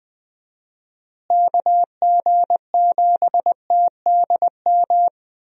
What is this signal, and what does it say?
Morse code, heard as a keyed tone, about 20 wpm, 700 Hz. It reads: KG7TDM
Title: USA Call Signs - 6 Characters - Repeated 2x - 20wpm